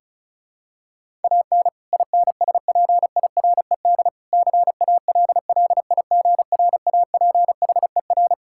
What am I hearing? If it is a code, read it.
Morse code, heard as a keyed tone, about 36 wpm, 700 Hz. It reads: AN INSPIRED CALLIGRAPHER
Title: Pangram Challenge v2 - 35wpm